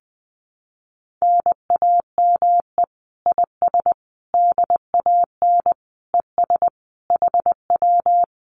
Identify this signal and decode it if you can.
Morse code, heard as a keyed tone, about 20 wpm, 700 Hz. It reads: NAME IS DAN ES HW
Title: Common QSO Phrases Encore 20wpm